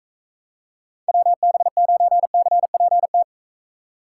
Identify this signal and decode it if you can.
Morse code, heard as a keyed tone, about 42 wpm, 700 Hz. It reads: WB9CPT